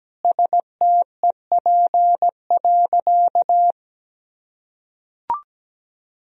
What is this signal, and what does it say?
Morse code, heard as a keyed tone, about 17 wpm, 700 Hz. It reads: STEP.
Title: Sentences from Top 1000 Words - Review Part 2 - 17wpm